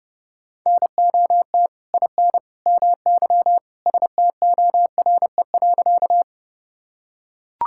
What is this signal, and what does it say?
Morse code, heard as a keyed tone, about 32 wpm, 700 Hz. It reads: NOT IN MY STORE.
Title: Sentences from Top 1000 Words - Review Part 5 - 30wpm